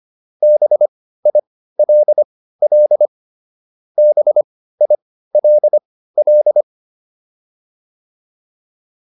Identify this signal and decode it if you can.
Morse code, heard as a keyed tone, about 25 wpm, spaced as for 15 wpm, 600 Hz. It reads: BILL BILL
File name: QSO401 15